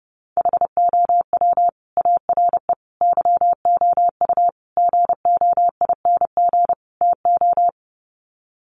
Morse code, heard as a keyed tone, about 30 wpm, 700 Hz. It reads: HOW ARE YOU GOING TO
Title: Sets of 5 Words 30wpm